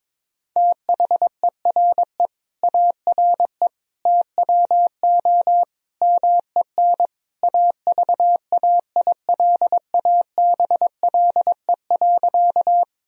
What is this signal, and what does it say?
Morse code, heard as a keyed tone, about 22 wpm, 700 Hz. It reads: THERE ARE TWO MEN AVAILABLE.
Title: Sentences from Top 1000 Words - Review Part 3 - 22wpm